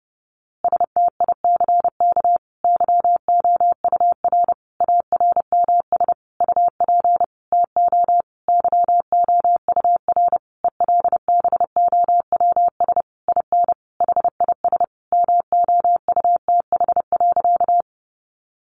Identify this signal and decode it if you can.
Morse code, heard as a keyed tone, about 30 wpm, 700 Hz. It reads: STICK YOUR ARMS UP TO YOUR ELBOWS IN HIS MOUTH.
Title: A Sound Of Thunder 30wpm